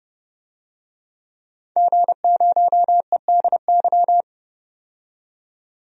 Morse code, heard as a keyed tone, about 30 wpm, 700 Hz. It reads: G0EDY
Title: DXCC-100 Call Signs - 5 Characters - 30wpm